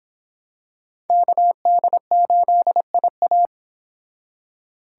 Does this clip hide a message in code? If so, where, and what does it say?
Yes: Morse code from 1.1 to 3.5 s, heard as a keyed tone, about 25 wpm, 700 Hz. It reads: KD8IA